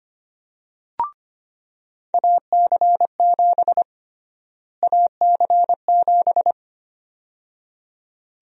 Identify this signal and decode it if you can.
Morse code, heard as a keyed tone, about 25 wpm, 700 Hz. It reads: AC7 AC7
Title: USA Call Sign Prefixes - Repeated 2x - 25wpm